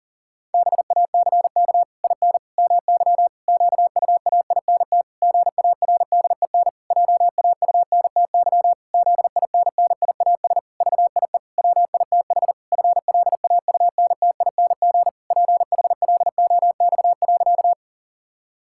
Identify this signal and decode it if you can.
Morse code, heard as a keyed tone, about 40 wpm, 700 Hz. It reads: BACK IN MY QUAINT GARDEN JAUNTY ZINNIAS VIE WITH FLAUNTING PHLOX.
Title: Pangram Challenge v2 - 40wpm